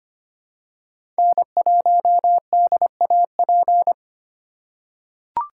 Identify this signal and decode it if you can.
Morse code, heard as a keyed tone, about 25 wpm, 700 Hz. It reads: N1DAP
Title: USA Call Signs - 4x Character Spacing - 25wpm